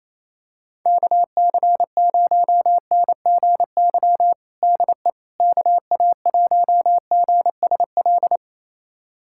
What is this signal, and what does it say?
Morse code, heard as a keyed tone, about 27 wpm, 700 Hz. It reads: KC0NGY DE KA1GSL
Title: USA Callsign DE Callsign - 6 Characters - 28wpm